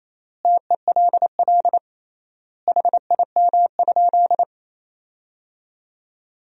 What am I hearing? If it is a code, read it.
Morse code, heard as a keyed tone, about 28 wpm, 700 Hz. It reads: TELL HIM?
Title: 3x Word-Spacing - Sentences from Top 200 Words - 28wpm